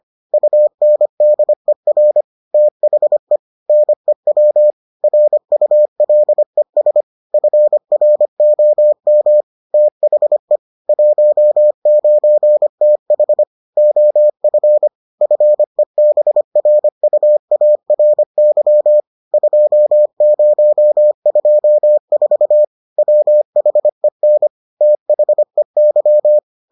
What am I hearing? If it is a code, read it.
Morse code, heard as a keyed tone, about 25 wpm, 600 Hz. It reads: UNDER THE NEW RULES FROM THE 19TH OF FEBRUARY 2024 WHEN THEY